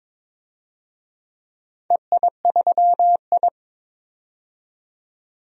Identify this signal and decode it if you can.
Morse code, heard as a keyed tone, about 22 wpm, 700 Hz. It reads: EI3I
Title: DXCC-100 Call Signs - 4 Characters - 22wpm